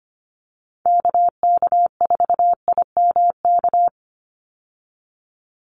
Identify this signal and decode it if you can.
Morse code, heard as a keyed tone, about 25 wpm, 700 Hz. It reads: KK4IMK
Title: USA Call Signs 25wpm